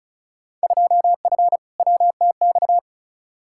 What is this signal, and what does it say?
Morse code, heard as a keyed tone, about 34 wpm, 700 Hz. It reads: 2F WTX
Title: Field Day Exchanges - 35wpm